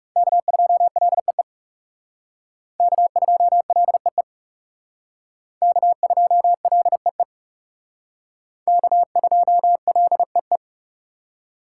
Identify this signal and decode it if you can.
Morse code, heard as a keyed tone, about 37 wpm, 700 Hz. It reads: K2LEE K2LEE K2LEE K2LEE
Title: Speed Racing - USA 1x3 Call Signs 30wpm